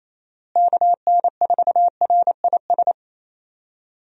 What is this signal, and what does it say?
Morse code, heard as a keyed tone, about 28 wpm, 700 Hz. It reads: KN4RIS
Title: USA Call Signs - 6 Characters - Repeated 3x - 28wpm